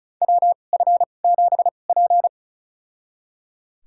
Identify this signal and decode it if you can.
Morse code, heard as a keyed tone, about 36 wpm, spaced as for 25 wpm, 700 Hz. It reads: WF7P
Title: USA Call Signs - 2x Character Spacing - 35wpm